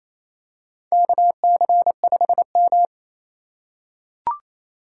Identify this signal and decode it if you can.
Morse code, heard as a keyed tone, about 28 wpm, 700 Hz. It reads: KC5M
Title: USA Call Signs - 4 Characters - 8x Character Spacing - 28wpm